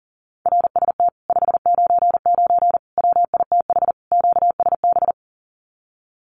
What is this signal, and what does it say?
Morse code, heard as a keyed tone, about 40 wpm, 700 Hz. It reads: RST 599 WITH QSB
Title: Common QSO Phrases Encore 40wpm